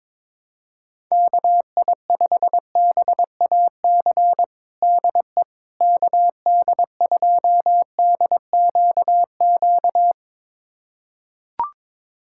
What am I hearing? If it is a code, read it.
Morse code, heard as a keyed tone, about 22 wpm, 700 Hz. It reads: KI5BAC DE KD2DQQ
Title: USA Callsign DE Callsign - 6 Characters - 22wpm